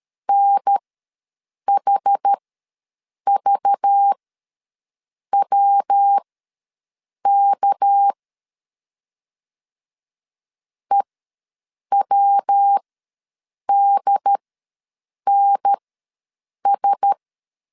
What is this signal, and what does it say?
Morse code, heard as a keyed tone, about 13 wpm, spaced as for 7 wpm, 800 Hz. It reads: NHVWK EWDNS
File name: LEZIONI / 6 PKX0 / 6 PKX0 ws 12wpm 3448